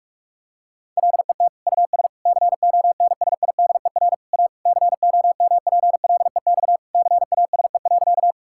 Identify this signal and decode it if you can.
Morse code, heard as a keyed tone, about 46 wpm, 700 Hz. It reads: LET US CONSIDER A COMPLEX CASE.
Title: Sentences from Top 1000 Words - Review Part 6 - 45wpm